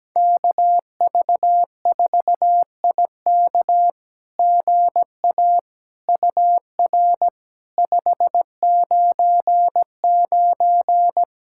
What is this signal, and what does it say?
Morse code, heard as a keyed tone, about 17 wpm, 700 Hz. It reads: KV4IK GA UR 599
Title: SOTA QSO Practice 17wpm